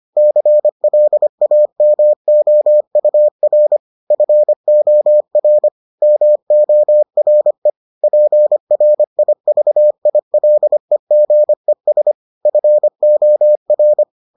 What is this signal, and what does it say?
Morse code, heard as a keyed tone, about 25 wpm, 600 Hz. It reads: CLAMOUR FOR MORE PRIVILEGES FOR